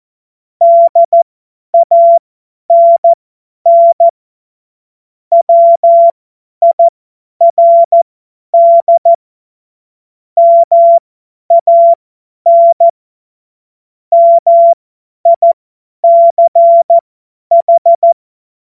Morse code, heard as a keyed tone, about 14 wpm, spaced as for 10 wpm, 680 Hz. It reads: DANN WIRD MAN MICH
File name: Morsekurs / Witze / Witz 06 Februar 2024 50 F70